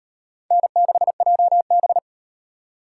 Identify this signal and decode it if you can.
Morse code, heard as a keyed tone, about 38 wpm, 700 Hz. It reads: N6JB